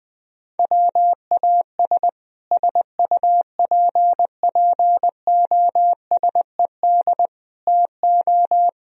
Morse code, heard as a keyed tone, about 21 wpm, 700 Hz. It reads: WAS SUPPOSED TO
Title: Speed-Racing - Sets of 4 Words 20wpm